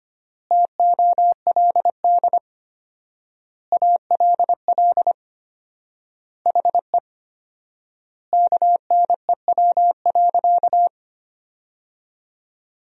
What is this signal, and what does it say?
Morse code, heard as a keyed tone, about 25 wpm, 700 Hz. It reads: TOLD ALL HE KNEW.
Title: 4x Word-Spacing - Sentences from Top 400 Words - 25wpm